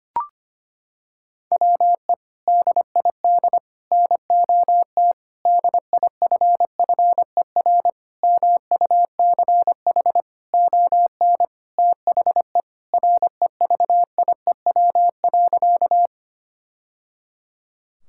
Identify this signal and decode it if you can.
Morse code, heard as a keyed tone, about 25 wpm, 700 Hz. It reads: WE DID NOT DIFFER MUCH ON THE REVIEW.
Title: Sentences from Top 1000 Words - Review Part 6 - 25wpm